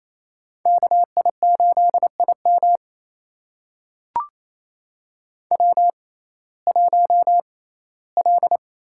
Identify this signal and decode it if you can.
Morse code, heard as a keyed tone, about 27 wpm, spaced as for 17 wpm, 700 Hz. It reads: KI8IM W1L
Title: USA Call Signs - 5 Characters - 6x Character Spacing - 28wpm